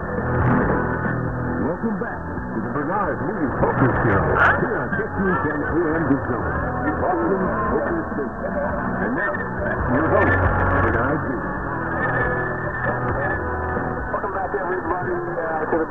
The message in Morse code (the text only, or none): none